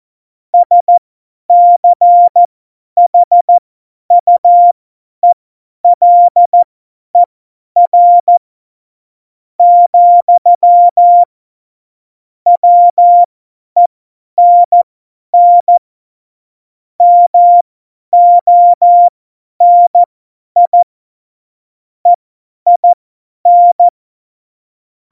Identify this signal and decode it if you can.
Morse code, heard as a keyed tone, about 14 wpm, spaced as for 10 wpm, 700 Hz. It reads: SCHUELER, WENN MONI EIN
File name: Morsekurs / Witze / Witz 21. Januar 2020, 50 F70